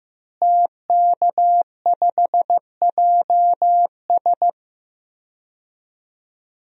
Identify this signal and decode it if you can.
Morse code, heard as a keyed tone, about 15 wpm, 700 Hz. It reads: TK5JS
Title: DXCC-100 Call Signs - 5 Characters - Repeated 3x - 15wpm